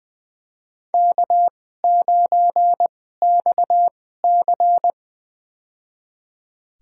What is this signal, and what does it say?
Morse code, heard as a keyed tone, about 20 wpm, spaced as for 14 wpm, 700 Hz. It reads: K9XC
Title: USA Call Signs - 4 Characters - 2x Character Spacing - 20wpm